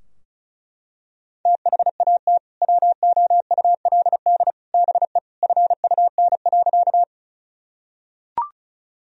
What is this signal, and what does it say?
Morse code, heard as a keyed tone, about 36 wpm, 700 Hz. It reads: THAT WOULD BE FUN.
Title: Sentences from Top 1000 Words - Review Part 4 - 35wpm